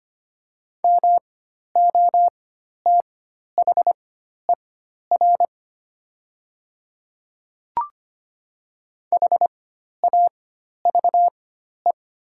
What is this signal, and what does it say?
Morse code, heard as a keyed tone, about 25 wpm, spaced as for 12 wpm, 700 Hz. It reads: MOTHER HAVE
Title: 4x Character Spacing - Top 300 Words - 25wpm